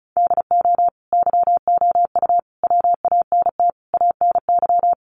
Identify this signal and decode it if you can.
Morse code, heard as a keyed tone, about 34 wpm, 700 Hz. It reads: DO YOU WANT ANY
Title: Sentences from Top 1400 Words 35wpm